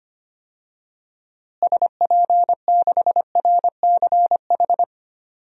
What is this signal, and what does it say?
Morse code, heard as a keyed tone, about 25 wpm, 700 Hz. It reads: SP6RCH
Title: DXCC-100 Call Signs - 6 Characters - Repeated 3x - 25wpm